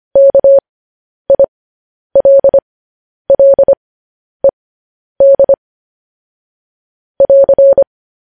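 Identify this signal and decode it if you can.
Morse code, heard as a keyed tone, about 25 wpm, spaced as for 10 wpm, 550 Hz. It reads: KILLED <AR>
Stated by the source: Morse news headlines sent at 10 wpm.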